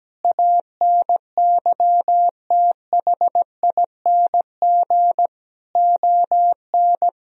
Morse code, heard as a keyed tone, about 17 wpm, 700 Hz. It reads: ANYTHING ON
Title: Sentences from Top 1000 Words - Review Part 4 - 17wpm